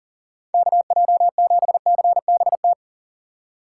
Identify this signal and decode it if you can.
Morse code, heard as a keyed tone, about 39 wpm, 700 Hz. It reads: KJ7CBT
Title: USA Call Signs - 6 Characters - 3x Character Spacing - 40wpm